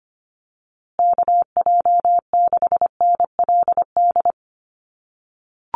Morse code, heard as a keyed tone, about 25 wpm, 700 Hz. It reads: KJ6NLD
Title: USA Call Signs Repeated 2x 25wpm